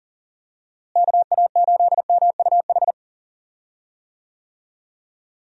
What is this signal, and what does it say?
Morse code, heard as a keyed tone, about 39 wpm, 700 Hz. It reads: KA8MUH
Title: USA Call Signs - 6 Characters - Repeated 3x - 40wpm